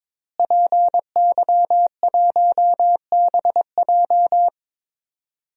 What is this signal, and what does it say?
Morse code, heard as a keyed tone, about 22 wpm, 700 Hz. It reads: PY1BJ
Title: DXCC-100 Call Signs - 5 Characters - Repeated 3x - 22wpm